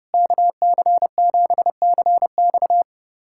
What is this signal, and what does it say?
Morse code, heard as a keyed tone, about 30 wpm, 700 Hz. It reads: KC7CX
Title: USA Callsign DE Callsign - 5 Characters - 30wpm